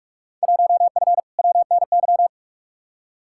Morse code, heard as a keyed tone, about 42 wpm, 700 Hz. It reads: 1F WNY
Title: Field Day Exchanges - 45wpm